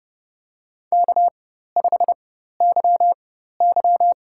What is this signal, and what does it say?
Morse code, heard as a keyed tone, about 30 wpm, spaced as for 14 wpm, 700 Hz. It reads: K5YY
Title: USA Call Signs - 4x Character Spacing - 30wpm